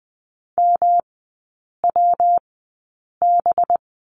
Morse code, heard as a keyed tone, about 20 wpm, spaced as for 8 wpm, 700 Hz. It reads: MWB
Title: 191 Lesson 31e - Letter v - Three Characters 20wpm